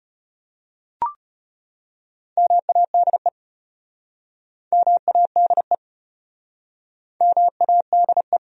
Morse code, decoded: MADE MADE MADE

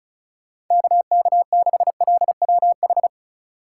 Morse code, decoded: KK6LWH

